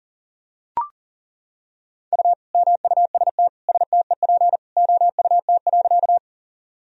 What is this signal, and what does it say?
Morse code, heard as a keyed tone, about 40 wpm, 700 Hz. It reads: U MUST STEP OUT.
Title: Sentences from Top 400 Words and Abbreviations 40wpm